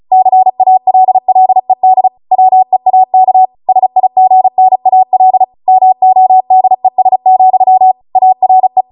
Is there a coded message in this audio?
Yes: Morse code from 0.1 to 8.8 s, heard as a keyed tone, about 36 wpm, 750 Hz. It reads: CALLED WEAK SIGNAL MODES, ARE